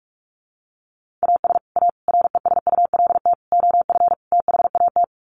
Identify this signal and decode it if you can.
Morse code, heard as a keyed tone, about 44 wpm, 700 Hz. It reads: AS A RESULT OF THAT
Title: Sets of 5 Words Encore 45wpm